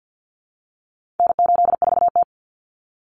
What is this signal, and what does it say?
Morse code, heard as a keyed tone, about 50 wpm, 700 Hz. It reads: N84T
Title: Common Cut Numbers - 4-Digit 50wpm